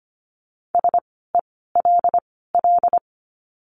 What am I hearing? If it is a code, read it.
Morse code, heard as a keyed tone, about 25 wpm, spaced as for 16 wpm, 700 Hz. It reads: SELL
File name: course / 25:10wpm / 094 Lesson 16f - Letter c - Words 25wpm